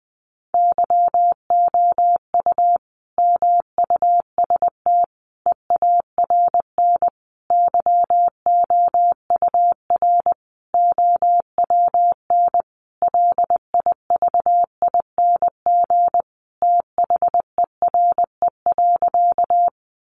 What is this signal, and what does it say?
Morse code, heard as a keyed tone, about 20 wpm, 700 Hz. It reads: YOU MUST EARN YOUR OWN LIVING THERE.
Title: Sentences from Top 1500 Words 20wpm